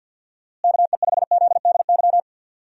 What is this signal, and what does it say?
Morse code, heard as a keyed tone, about 51 wpm, 700 Hz. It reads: KE5ZDY
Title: USA Call Signs - 6 Characters - Repeated 3x - 50wpm